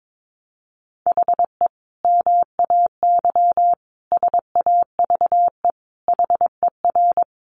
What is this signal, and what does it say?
Morse code, heard as a keyed tone, about 22 wpm, 700 Hz. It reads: HE MAY SAVE HER
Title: Sentences from Top 1100 Words 22wpm